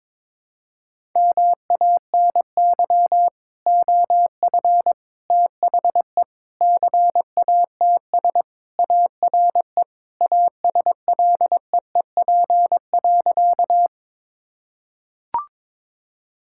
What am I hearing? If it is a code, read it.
Morse code, heard as a keyed tone, about 22 wpm, 700 Hz. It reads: MANY OF THE CATS ARE ASLEEP.
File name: practice / 22wpm / Sentences with 3-Word Phrases - 22wpm